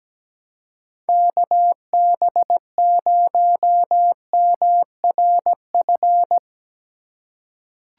Morse code, heard as a keyed tone, about 17 wpm, 700 Hz. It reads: KB0MRF
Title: USA Call Signs - 2x Character Spacing - 17wpm